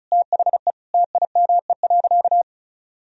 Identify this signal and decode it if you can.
Morse code, heard as a keyed tone, about 34 wpm, 700 Hz. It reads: THE TIME.
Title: Sentences from Top 1000 Words - Review Part 4 - 35wpm